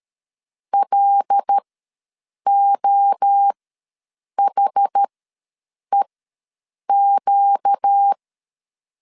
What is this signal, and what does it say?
Morse code, heard as a keyed tone, about 13 wpm, spaced as for 7 wpm, 800 Hz. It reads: LOHEQ
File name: LEZIONI / 5 LBVC / 5 L ws 12wpm 2958